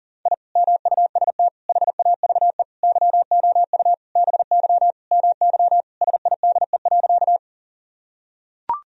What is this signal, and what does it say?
Morse code, heard as a keyed tone, about 40 wpm, 700 Hz. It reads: I MUST HAVE YOU BY MY SIDE.
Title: Sentences from Top 1000 Words - Review Part 1 - 40wpm